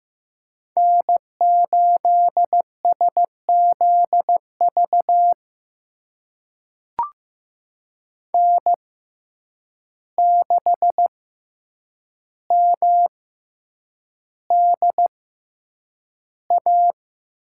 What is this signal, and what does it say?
Morse code, heard as a keyed tone, about 15 wpm, spaced as for 8 wpm, 700 Hz. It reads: N8SZV N6MDA
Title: USA Call Signs - 6x Character Spacing - 15wpm